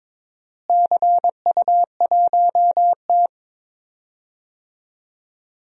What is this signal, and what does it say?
Morse code, heard as a keyed tone, about 22 wpm, 700 Hz. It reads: CU1T
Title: DXCC-100 Call Signs - 4 Characters - Repeated 3x - 22wpm